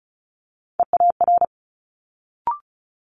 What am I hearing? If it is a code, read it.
Morse code, heard as a keyed tone, about 36 wpm, 700 Hz. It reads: EAR
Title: Multi-letter Phonemes Quick Fire 35wpm